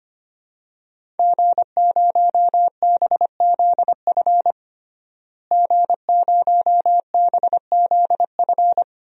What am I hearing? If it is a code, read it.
Morse code, heard as a keyed tone, about 25 wpm, 700 Hz. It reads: G0BZF G0BZF
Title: DXCC-100 Call Signs - 5 Characters - Repeated 2x - 25wpm